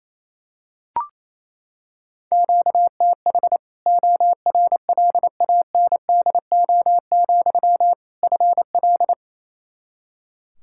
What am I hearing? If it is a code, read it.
Morse code, heard as a keyed tone, about 28 wpm, 700 Hz. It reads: QTH ORLANDO, FL
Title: Practice QSO - Exchange 1 - Beginning - 28wpm